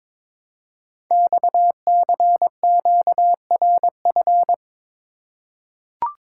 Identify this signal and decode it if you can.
Morse code, heard as a keyed tone, about 22 wpm, 700 Hz. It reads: XCQRF
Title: Code Groups - 5 Letters - 22wpm